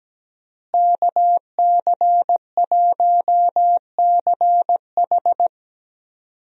Morse code, heard as a keyed tone, about 17 wpm, 700 Hz. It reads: KC1CH
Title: USA Call Signs - 5 Characters - Repeated 3x - 17wpm